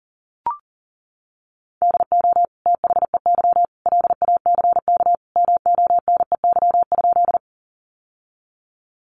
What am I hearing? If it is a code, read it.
Morse code, heard as a keyed tone, about 40 wpm, 700 Hz. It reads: DO THEY LACK MONEY?
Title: Sentences from Top 1100 Words 40wpm